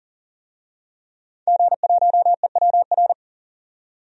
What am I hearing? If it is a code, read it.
Morse code, heard as a keyed tone, about 40 wpm, 700 Hz. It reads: G1EWR
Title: DXCC-100 Call Signs - 5 Characters - 40wpm